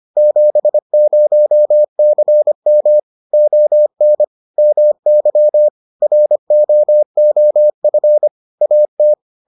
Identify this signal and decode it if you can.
Morse code, heard as a keyed tone, about 25 wpm, 600 Hz. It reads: 70CM ON MY ROOF AT